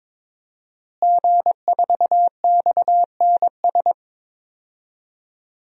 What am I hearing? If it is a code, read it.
Morse code, heard as a keyed tone, about 21 wpm, 700 Hz. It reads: G4XNS
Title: DXCC-100 Call Signs - 5 Characters - Repeated 2x - 22wpm